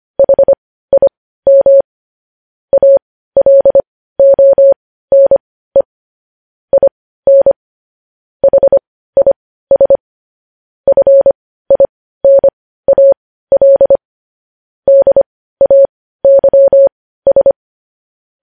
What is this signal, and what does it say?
Morse code, heard as a keyed tone, about 25 wpm, spaced as for 15 wpm, 550 Hz. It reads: HIM ALONE IN HIS FINAL DAYS
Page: News headlines in Morse code at 15 wpm